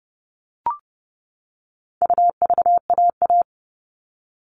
Morse code, heard as a keyed tone, about 30 wpm, 700 Hz. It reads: UVAA
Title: All Cut Numbers - 4-Digit 30wpm